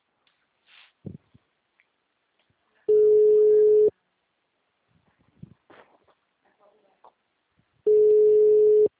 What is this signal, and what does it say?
Ringback tone (Europe, 425 Hz), with speech.